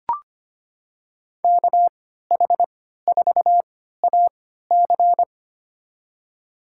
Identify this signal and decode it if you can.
Morse code, heard as a keyed tone, about 25 wpm, spaced as for 14 wpm, 700 Hz. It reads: KH4AC
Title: USA Call Signs - 5 Characters - 3x Character Spacing - 25wpm